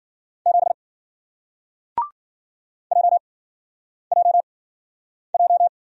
Morse code, heard as a keyed tone, about 54 wpm, spaced as for 9 wpm, 700 Hz. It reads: 7 JJJ